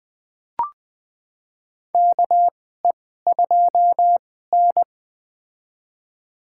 Morse code, heard as a keyed tone, about 20 wpm, spaced as for 14 wpm, 700 Hz. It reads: KE2N
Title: USA Call Signs - 4 Characters - 2x Character Spacing - 20wpm